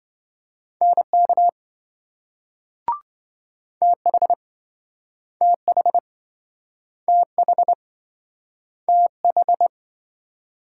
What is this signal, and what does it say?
Morse code, heard as a keyed tone, about 27 wpm, 700 Hz. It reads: NK TH TH TH TH